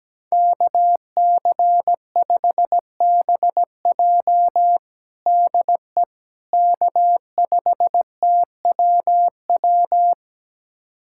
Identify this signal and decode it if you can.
Morse code, heard as a keyed tone, about 17 wpm, 700 Hz. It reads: KC5BJ DE K5TWW
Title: USA Callsign DE Callsign - 5 Characters - 17wpm